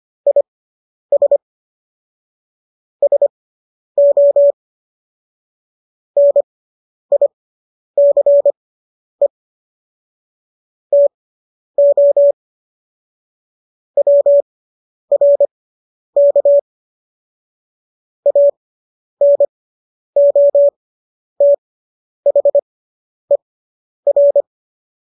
Morse code, heard as a keyed tone, about 25 wpm, spaced as for 10 wpm, 570 Hz. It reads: IS SO NICE TO WRK ANOTHER